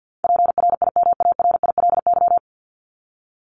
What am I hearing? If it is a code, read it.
Morse code, heard as a keyed tone, about 50 wpm, 700 Hz. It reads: PRIMARILY
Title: LY Suffix - Top 100 Words with Base Words 50wpm